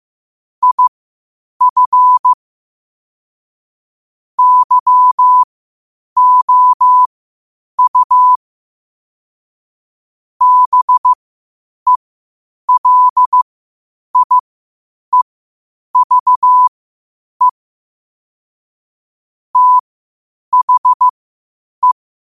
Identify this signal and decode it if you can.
Morse code, heard as a keyed tone, about 15 wpm, spaced as for 8 wpm, 1000 Hz. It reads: IF YOU BELIEVE THE
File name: qotdfiles / 8wpm